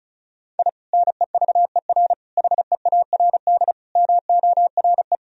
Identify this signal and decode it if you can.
Morse code, heard as a keyed tone, about 34 wpm, 700 Hz. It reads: I NEVER HEARD MORE
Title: Sentences from Top 1000 Words - Review Part 1 - 35wpm